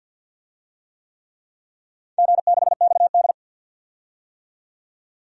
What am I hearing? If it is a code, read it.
Morse code, heard as a keyed tone, about 48 wpm, 700 Hz. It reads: G6XD
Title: DXCC-100 Call Signs - 4 Characters - 50wpm